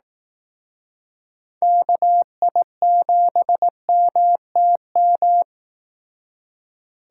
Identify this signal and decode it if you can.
Morse code, heard as a keyed tone, about 18 wpm, 700 Hz. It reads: KI7MTM
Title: Speed Racing - USA 2x3 Call Signs 15wpm